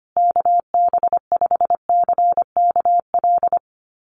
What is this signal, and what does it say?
Morse code, heard as a keyed tone, about 25 wpm, 700 Hz. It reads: KB5CKL
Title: USA Call Signs Repeated 3x 25wpm